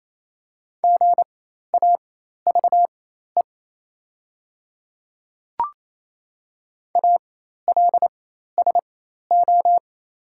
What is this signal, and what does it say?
Morse code, heard as a keyed tone, about 28 wpm, spaced as for 13 wpm, 700 Hz. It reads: GAVE ALSO
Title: 4x Character Spacing - Top 500 Words - 28wpm